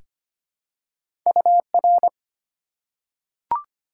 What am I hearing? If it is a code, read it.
Morse code, heard as a keyed tone, about 25 wpm, 700 Hz. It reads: UR
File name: Top 90 QSO Elements - 2x Character Spacing - 25wpm